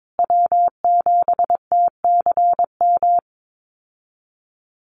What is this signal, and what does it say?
Morse code, heard as a keyed tone, about 22 wpm, 700 Hz. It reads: W7TCM
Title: USA Call Signs Repeated 2x 22wpm